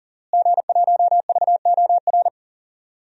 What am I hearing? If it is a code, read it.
Morse code, heard as a keyed tone, about 37 wpm, 700 Hz. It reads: G1VOR
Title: DXCC-100 Call Signs - 5 Characters - Repeated 2x - 40wpm